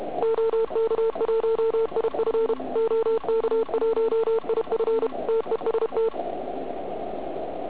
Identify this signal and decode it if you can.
Morse code, heard as a keyed tone, about 33 wpm, 440 Hz, with noise. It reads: OK1IF OK1IF TEST
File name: Archiv / CUC106 / cuc106 PILEUP